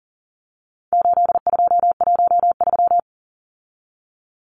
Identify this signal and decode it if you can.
Morse code, heard as a keyed tone, about 40 wpm, 700 Hz. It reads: 8213
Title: Numbers - 4-Digit 40wpm